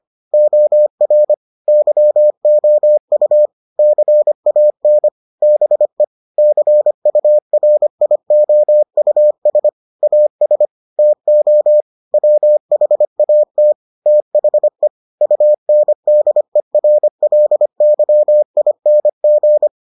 Morse code, heard as a keyed tone, about 25 wpm, 600 Hz. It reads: OR YOU CAN BE CURIOUS AS TO WHAT THE UNDERLYING